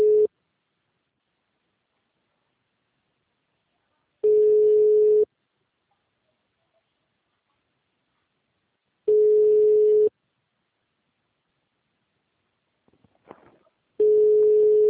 Ringback tone (Europe, 425 Hz).
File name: out- -2002- - - .31966